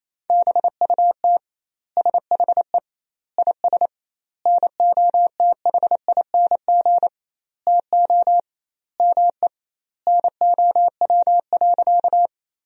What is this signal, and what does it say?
Morse code, heard as a keyed tone, about 28 wpm, 700 Hz. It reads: BUT SHE IS NOTHING TO ME NOW.
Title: 2x Word-Spacing - Sentences from Top 500 Words - 28wpm